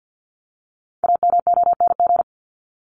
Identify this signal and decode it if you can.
Morse code, heard as a keyed tone, about 50 wpm, 700 Hz. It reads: AMONG
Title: Top 500 Words Repeated 50wpm